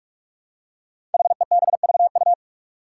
Morse code, heard as a keyed tone, about 43 wpm, 700 Hz. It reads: HEBVU
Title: Five Challenging Letter-Number Combinations 45wpm